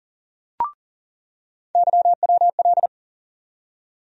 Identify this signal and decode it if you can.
Morse code, heard as a keyed tone, about 40 wpm, 700 Hz. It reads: YWL
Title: USA Call Sign Postfixes - 3 Letters 40wpm